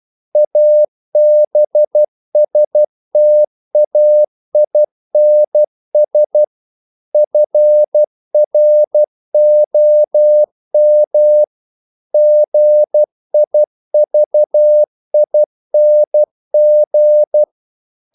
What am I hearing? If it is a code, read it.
Morse code, heard as a keyed tone, about 12 wpm, 600 Hz. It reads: ABSTAINS FROM GIVING
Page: English quotations in CW (telegraphy - morse)